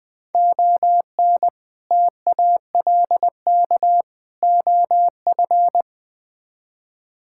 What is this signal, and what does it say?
Morse code, heard as a keyed tone, about 20 wpm, 690 Hz. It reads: ON TALK OF
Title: Build A Phrase - 5-Word Phrases - 20wpm